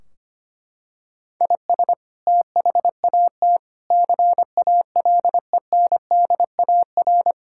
Morse code, heard as a keyed tone, about 25 wpm, 700 Hz. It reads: IS THAT CALENDAR
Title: Sentences from Top 1600 Words 25wpm